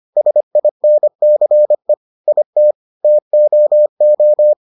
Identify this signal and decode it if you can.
Morse code, heard as a keyed tone, about 25 wpm, 600 Hz. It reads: SINCE IT TOO